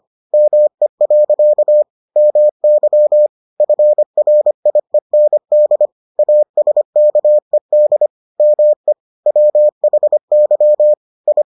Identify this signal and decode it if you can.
Morse code, heard as a keyed tone, about 25 wpm, 600 Hz. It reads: ME. MY FRIEND ASKED ME WHY I